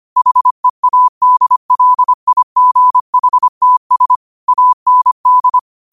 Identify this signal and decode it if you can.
Morse code, heard as a keyed tone, about 25 wpm, 1000 Hz. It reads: HEADLIGHTS AND
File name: qotdfiles / 25wpm